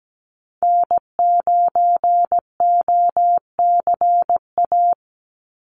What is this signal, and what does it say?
Morse code, heard as a keyed tone, about 17 wpm, 700 Hz. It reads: N9OCA
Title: USA Call Signs 17wpm